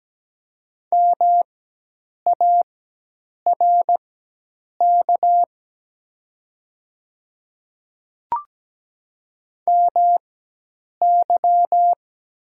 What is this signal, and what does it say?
Morse code, heard as a keyed tone, about 17 wpm, spaced as for 8 wpm, 700 Hz. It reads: MARK MY
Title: 4x Character Spacing - Top 500 Words - 17wpm